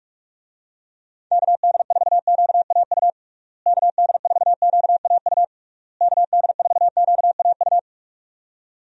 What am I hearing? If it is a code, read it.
Morse code, heard as a keyed tone, about 41 wpm, 700 Hz. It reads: KD4QAU KD4QAU KD4QAU
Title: USA Call Signs - 6 Characters - Repeated 3x - 45wpm